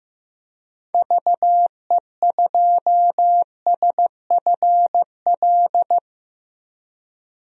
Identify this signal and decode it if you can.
Morse code, heard as a keyed tone, about 15 wpm, 700 Hz. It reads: VE2SFL
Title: DXCC-100 Call Signs - 6 Characters - Repeated 3x - 15wpm